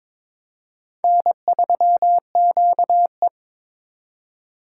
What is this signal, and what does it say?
Morse code, heard as a keyed tone, about 21 wpm, 700 Hz. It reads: N3QE
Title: K1USN SST QSO Practice - 22wpm